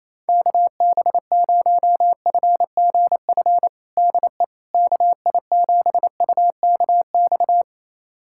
Morse code, heard as a keyed tone, about 28 wpm, 700 Hz. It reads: KB0FGF DE KI7UKX
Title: USA Callsign DE Callsign - 6 Characters - 28wpm